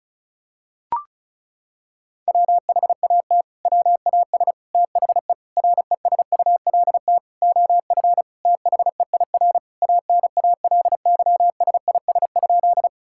Morse code, heard as a keyed tone, about 36 wpm, 700 Hz. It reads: WHAT WAS THE RESULT OF THEIR ANALYSIS?
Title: Sentences from Top 1000 Words - Review Part 5 - 35wpm